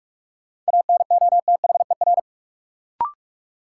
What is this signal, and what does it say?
Morse code, heard as a keyed tone, about 46 wpm, 700 Hz. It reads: ANOTHER
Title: Sentences from Top 600 Words - New Word Drill - 45wpm